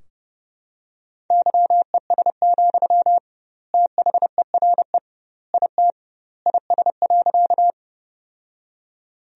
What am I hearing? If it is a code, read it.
Morse code, heard as a keyed tone, about 30 wpm, 700 Hz. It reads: YES, THERE IT IS.
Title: 2x Word-Spacing - Sentences from Top 500 Words - 30wpm